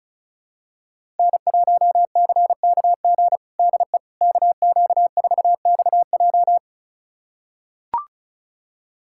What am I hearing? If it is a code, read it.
Morse code, heard as a keyed tone, about 36 wpm, 700 Hz. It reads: N1CKG DE KQ4XJ
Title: USA Callsign DE Callsign - 5 Characters - 35wpm